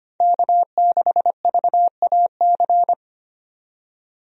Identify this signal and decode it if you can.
Morse code, heard as a keyed tone, about 25 wpm, 700 Hz. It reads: K6VAC
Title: USA Call Signs - 5 Characters - 25wpm